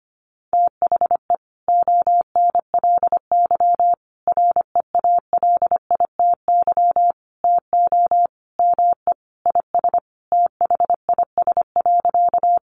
Morse code, heard as a keyed tone, about 25 wpm, 700 Hz. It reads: THE ONLY REALITY TO ME IS THIS.
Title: Sentences from Top 1200 Words 25wpm